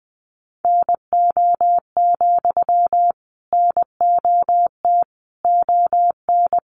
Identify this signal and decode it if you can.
Morse code, heard as a keyed tone, about 20 wpm, 700 Hz. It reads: NO, NOT ON
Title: Sentences from Top 900 Words 20wpm